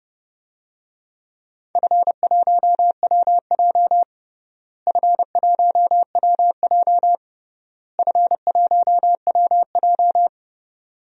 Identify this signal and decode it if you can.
Morse code, heard as a keyed tone, about 29 wpm, 700 Hz. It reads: F1WJ F1WJ F1WJ
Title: DXCC-100 Call Signs - 4 Characters - Repeated 3x - 30wpm